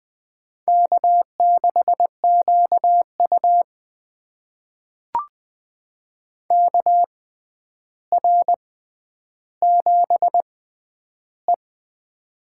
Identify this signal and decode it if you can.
Morse code, heard as a keyed tone, about 20 wpm, spaced as for 10 wpm, 700 Hz. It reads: K6QU KR7E
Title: USA Call Signs - 4 Characters - 6x Character Spacing - 20wpm